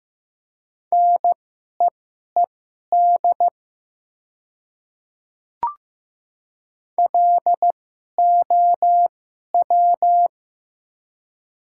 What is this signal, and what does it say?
Morse code, heard as a keyed tone, about 15 wpm, spaced as for 11 wpm, 700 Hz. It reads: NEED LOW
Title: 2x Character Spacing - Top 200 Words - 15wpm